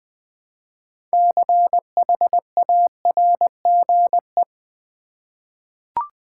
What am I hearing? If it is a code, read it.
Morse code, heard as a keyed tone, about 21 wpm, 700 Hz. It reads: CHARGE
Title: Six Letter Words - 20wpm